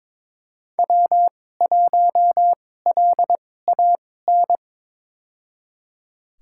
Morse code, heard as a keyed tone, about 22 wpm, spaced as for 16 wpm, 700 Hz. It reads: W1LAN
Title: USA Call Signs - 5 Characters - 2x Character Spacing - 22wpm